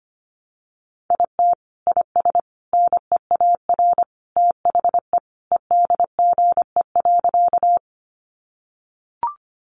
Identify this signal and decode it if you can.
Morse code, heard as a keyed tone, about 25 wpm, 700 Hz. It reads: IT IS NEAR THE EDGE.
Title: Additional Easy Phrases 25wpm